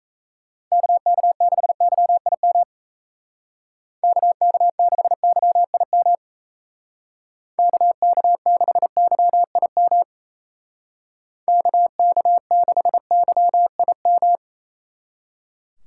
Morse code, heard as a keyed tone, about 35 wpm, 700 Hz. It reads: KK6YIM KK6YIM KK6YIM KK6YIM